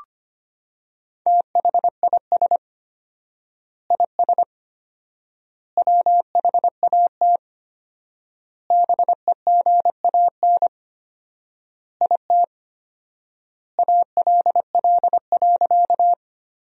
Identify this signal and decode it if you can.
Morse code, heard as a keyed tone, about 25 wpm, 700 Hz. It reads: THIS IS WHAT BEGAN IT ALL.
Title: 4x Word-Spacing - Sentences from Top 300 Words - 25wpm